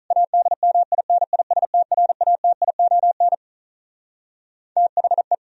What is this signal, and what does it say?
Morse code, heard as a keyed tone, about 47 wpm, 700 Hz. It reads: ADMINISTRATION THE